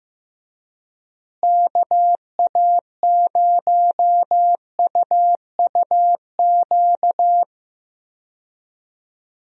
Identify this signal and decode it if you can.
Morse code, heard as a keyed tone, about 15 wpm, 700 Hz. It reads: KA0UUQ
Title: USA Call Signs - 6 Characters - Repeated 3x - 15wpm